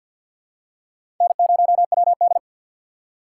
Morse code, heard as a keyed tone, about 49 wpm, 700 Hz. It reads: N0WD
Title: USA Call Signs - 4 Characters - 3x Character Spacing - 50wpm